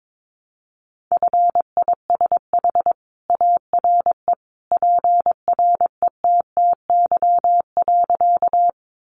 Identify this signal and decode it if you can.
Morse code, heard as a keyed tone, about 22 wpm, 700 Hz. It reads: FISH ARE PRETTY.
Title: Easy Phrases 22wpm